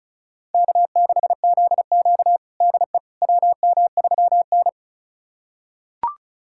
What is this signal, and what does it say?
Morse code, heard as a keyed tone, about 36 wpm, 700 Hz. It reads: K6ZQ DE WM3N
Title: USA Callsign DE Callsign - 4 Characters - 35wpm